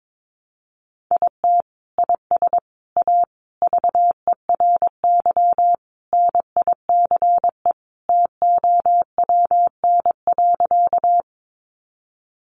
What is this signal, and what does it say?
Morse code, heard as a keyed tone, about 22 wpm, 700 Hz. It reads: IT IS A VERY NICE TOWN.
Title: Sentences from Top 1100 Words 22wpm